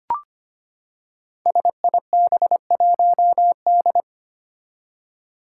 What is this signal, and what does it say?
Morse code, heard as a keyed tone, about 25 wpm, 700 Hz. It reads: SIB1D
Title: Five Challenging Letter-Number Combinations 25wpm